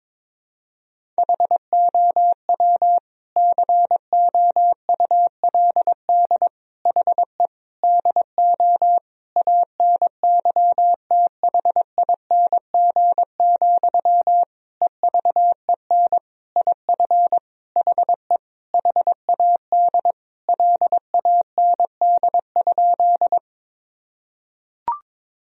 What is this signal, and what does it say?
Morse code, heard as a keyed tone, about 22 wpm, 700 Hz. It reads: HOW COULD HE DO ANYTHING, EVEN IF HE HAD LAND?
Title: Sentences from Top 1000 Words - Review Part 1 - 22wpm